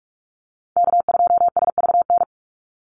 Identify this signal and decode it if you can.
Morse code, heard as a keyed tone, about 44 wpm, 700 Hz. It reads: K2SVN
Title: USA Call Signs 45wpm